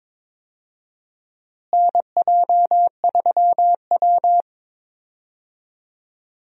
Morse code, heard as a keyed tone, about 22 wpm, 700 Hz. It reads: NJ3W
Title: USA Call Signs - 4 Characters - Repeated 2x - 22wpm